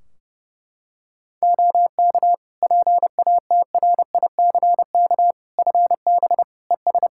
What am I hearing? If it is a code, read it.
Morse code, heard as a keyed tone, about 30 wpm, 700 Hz. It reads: OK PATRICK FB ES
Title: Practice QSO - Exchange 2 - Info - 30wpm